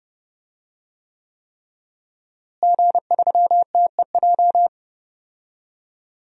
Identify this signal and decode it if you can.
Morse code, heard as a keyed tone, about 30 wpm, 700 Hz. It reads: G3TEJ
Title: DXCC-100 Call Signs - 5 Characters - Repeated 3x - 30wpm